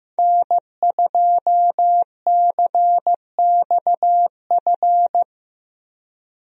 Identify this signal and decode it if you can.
Morse code, heard as a keyed tone, about 16 wpm, 700 Hz. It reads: N2CXF
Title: USA Call Signs - 5 Characters - Repeated 3x - 15wpm